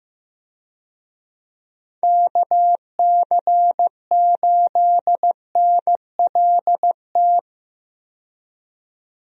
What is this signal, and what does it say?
Morse code, heard as a keyed tone, about 15 wpm, 700 Hz. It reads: KC8NLT
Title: USA Call Signs - 6 Characters - 15wpm